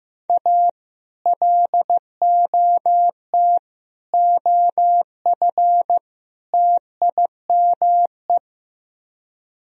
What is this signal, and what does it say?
Morse code, heard as a keyed tone, about 15 wpm, 700 Hz. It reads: A LOT OF TIME
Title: Speed-Racing - Sets of 4 Words 15wpm